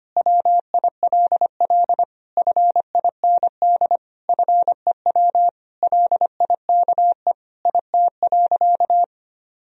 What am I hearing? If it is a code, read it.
Morse code, heard as a keyed tone, about 25 wpm, 700 Hz. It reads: WILL FIND FEW LIKE IT.
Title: Sentences from Top 1000 Words - Review Part 1 - 25wpm